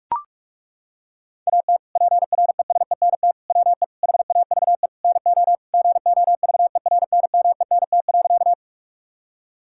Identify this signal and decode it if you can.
Morse code, heard as a keyed tone, about 41 wpm, 700 Hz. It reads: AT PRESENT WE HAVE NO GOVERNMENT.
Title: Sentences from Top 1000 Words - Review Part 5 - 45wpm